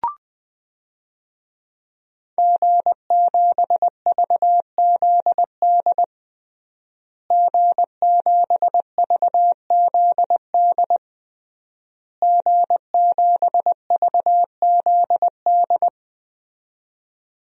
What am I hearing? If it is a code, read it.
Morse code, heard as a keyed tone, about 21 wpm, 700 Hz. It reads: G7VZD G7VZD G7VZD
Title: DXCC-100 Call Signs - 5 Characters - Repeated 3x - 20wpm